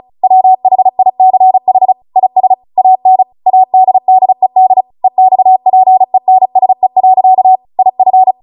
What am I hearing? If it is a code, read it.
Morse code, heard as a keyed tone, about 36 wpm, 750 Hz. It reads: WHICH IS AN ADDED EXPENSE. IF
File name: Morse / 35WPM